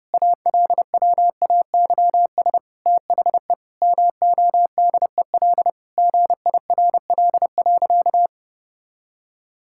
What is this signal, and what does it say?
Morse code, heard as a keyed tone, about 32 wpm, 700 Hz. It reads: ALWAYS THE MODEL GIRL.
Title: Sentences from Top 1000 Words - Review Part 5 - 30wpm